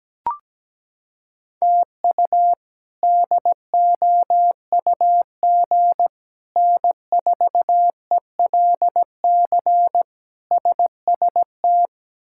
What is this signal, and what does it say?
Morse code, heard as a keyed tone, about 17 wpm, 700 Hz. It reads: TU DOUG N4ELC SST
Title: K1USN SST QSO Practice - 17wpm